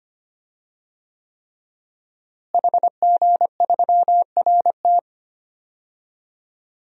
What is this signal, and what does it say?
Morse code, heard as a keyed tone, about 25 wpm, 700 Hz. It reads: HG3RT
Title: DXCC-100 Call Signs - 5 Characters - Repeated 3x - 25wpm